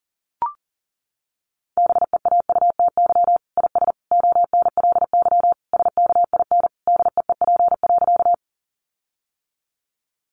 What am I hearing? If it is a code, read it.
Morse code, heard as a keyed tone, about 40 wpm, 700 Hz. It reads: BEAUTY IS ONLY SKIN DEEP.
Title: Sentences from Top 1300 Words 40wpm